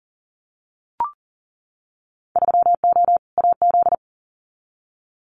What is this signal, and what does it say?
Morse code, heard as a keyed tone, about 40 wpm, 700 Hz. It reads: 3O AZ
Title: Winter Field Day Exchanges 40wpm